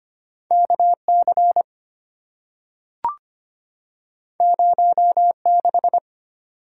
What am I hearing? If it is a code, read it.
Morse code, heard as a keyed tone, about 25 wpm, 700 Hz. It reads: KC 06
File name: Two Letter-Number Combinations - 25wpm